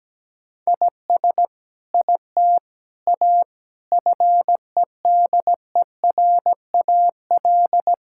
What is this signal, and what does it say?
Morse code, heard as a keyed tone, about 17 wpm, 700 Hz. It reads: IS IT A FEDERAL
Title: Sentences from Top 1000 Words - Review Part 5 - 17wpm